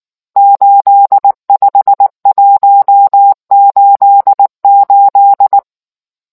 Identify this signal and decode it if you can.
Morse code, heard as a keyed tone, about 19 wpm, 800 Hz. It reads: 85188